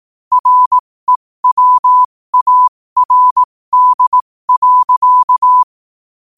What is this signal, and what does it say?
Morse code, heard as a keyed tone, about 18 wpm, spaced as for 15 wpm, 1000 Hz. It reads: REWARD.
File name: qotdfiles / 15wpm